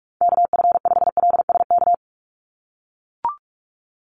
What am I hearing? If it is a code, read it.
Morse code, heard as a keyed tone, about 44 wpm, 700 Hz. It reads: KF5LSK
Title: USA Call Signs 45wpm